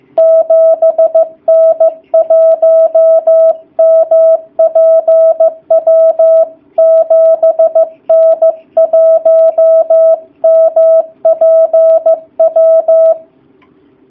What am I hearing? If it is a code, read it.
Morse code, heard as a keyed tone, about 15 wpm, 660 Hz. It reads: 7N1MPW 7N1MPW
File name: jh8cbh / 2019cw / sp15- w